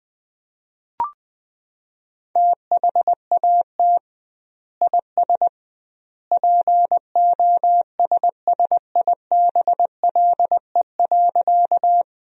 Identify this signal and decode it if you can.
Morse code, heard as a keyed tone, about 21 wpm, 700 Hz. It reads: THAT IS POSSIBLE.
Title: 2x Word-Spacing - Sentences from Top 500 Words - 20wpm